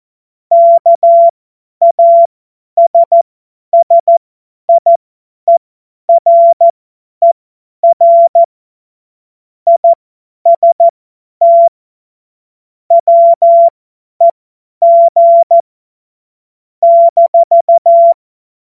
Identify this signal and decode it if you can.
Morse code, heard as a keyed tone, about 14 wpm, spaced as for 10 wpm, 680 Hz. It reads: KASSIERER IST WEG -
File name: Morsekurs / Witze / Witz 29 April 2025 50 F70